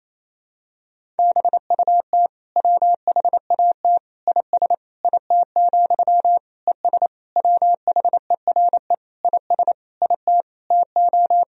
Morse code, heard as a keyed tone, about 28 wpm, 700 Hz. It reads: BUT WHAT IS IT, ES WHERE IS IT TO
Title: Sentences from Top 300 Words and Abbreviations 28wpm